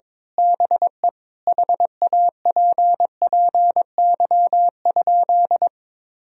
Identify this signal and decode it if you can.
Morse code, heard as a keyed tone, about 22 wpm, 700 Hz. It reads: BE HAPPY?
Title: Sentences from Top 1000 Words - Review Part 2 - 22wpm